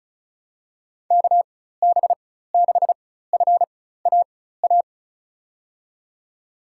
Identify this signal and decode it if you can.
Morse code, heard as a keyed tone, about 36 wpm, spaced as for 16 wpm, 700 Hz. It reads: KB6FAA
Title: USA Call Signs - 6 Characters - 4x Character Spacing - 35wpm